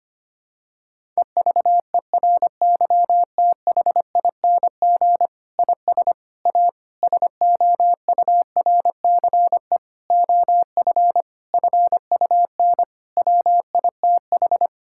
Morse code, heard as a keyed tone, about 25 wpm, 700 Hz. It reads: EVERYTHING IS A SOURCE OF FUN WITH
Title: Sentences from Top 1000 Words - Review Part 5 - 25wpm